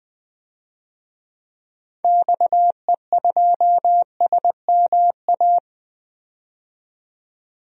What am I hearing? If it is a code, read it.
Morse code, heard as a keyed tone, about 20 wpm, 700 Hz. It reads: XE2SMA
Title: DXCC-100 Call Signs - 6 Characters - 20wpm